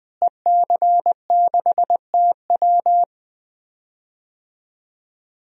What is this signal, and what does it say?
Morse code, heard as a keyed tone, about 20 wpm, 700 Hz. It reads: EC6TW
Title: DXCC-100 Call Signs - 5 Characters - Repeated 3x - 20wpm